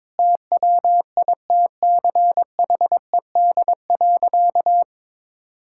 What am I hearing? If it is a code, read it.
Morse code, heard as a keyed tone, about 21 wpm, 700 Hz. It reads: TWITCHED.
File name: practice / 22wpm / Sentences with 2-Word Phrases - 22wpm